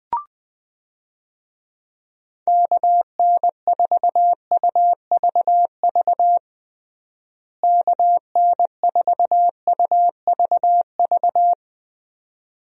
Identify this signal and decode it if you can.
Morse code, heard as a keyed tone, about 21 wpm, 700 Hz. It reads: KN4UVV KN4UVV
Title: USA Call Signs - 6 Characters - Repeated 2x - 20wpm